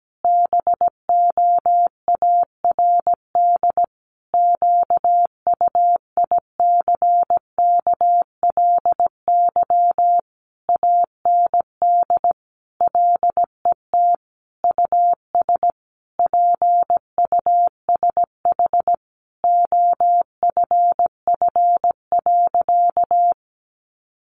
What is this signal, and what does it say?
Morse code, heard as a keyed tone, about 17 wpm, 700 Hz. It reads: BOARD QUICKLY AND LET US PUSH OFF.
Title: Sentences from Top 1300 Words 17wpm